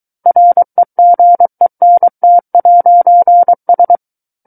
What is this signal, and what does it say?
Morse code, heard as a keyed tone, about 23 wpm, 700 Hz. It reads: REGENT'S
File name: Book One / 14-Chapter 14 - 23 WPM